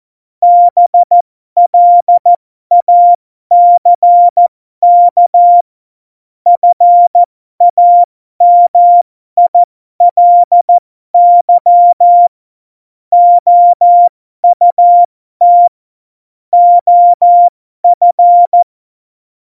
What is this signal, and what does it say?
Morse code, heard as a keyed tone, about 14 wpm, spaced as for 12 wpm, 700 Hz. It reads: BLACK FAMILY OUT OF
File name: Morsekurs / Witze